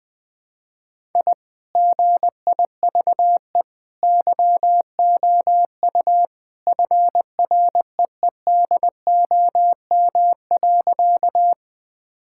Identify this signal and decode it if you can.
Morse code, heard as a keyed tone, about 20 wpm, 700 Hz. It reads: I GIVE YOU FREEDOM.
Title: Sentences from Top 1000 Words - Review Part 5 - 20wpm